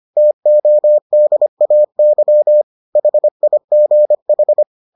Morse code, heard as a keyed tone, about 25 wpm, 600 Hz. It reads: TODAY HIGH